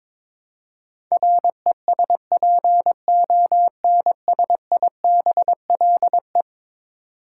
Morse code, heard as a keyed tone, about 22 wpm, 700 Hz. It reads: RESPONSIBLE
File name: practice / 22wpm / Eleven Letter Words - 22wpm